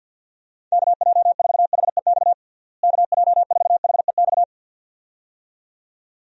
Morse code, heard as a keyed tone, about 50 wpm, 700 Hz. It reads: KJ4HEX KJ4HEX
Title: USA Call Signs - 6 Characters - Repeated 2x - 50wpm